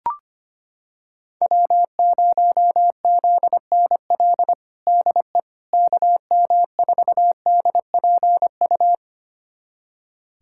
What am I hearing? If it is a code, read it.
Morse code, heard as a keyed tone, about 25 wpm, 700 Hz. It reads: W0ZNL DE KM4DPU
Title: USA Callsign DE Callsign 25wpm